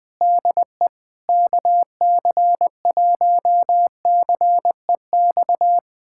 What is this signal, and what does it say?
Morse code, heard as a keyed tone, about 21 wpm, 700 Hz. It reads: DE KC1CEX
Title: USA Callsign DE Callsign - 6 Characters - 20wpm